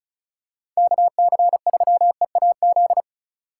Morse code, heard as a keyed tone, about 36 wpm, 700 Hz. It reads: KC3EAZ
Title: USA Call Signs - 6 Characters - 6x Character Spacing - 35wpm